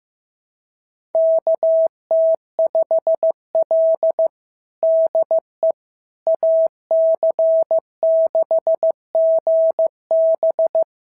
Morse code, heard as a keyed tone, about 15 wpm, 650 Hz. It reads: KT5L DE AC6GB
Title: SOTA QSO Practice 15wpm